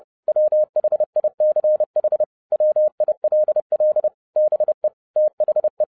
Morse code, heard as a keyed tone, about 30 wpm, 610 Hz. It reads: WHICH WILL BE THE
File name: 2012-01-mp3 / n12-I DIDNT MISS THE SOLAR CYCLE 24-30wpm